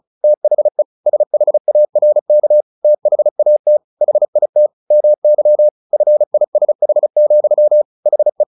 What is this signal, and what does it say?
Morse code, heard as a keyed tone, about 34 wpm, 600 Hz. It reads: THE SHARK THAT HIT MY FISH, HE
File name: the old man and the sea 2 35wpm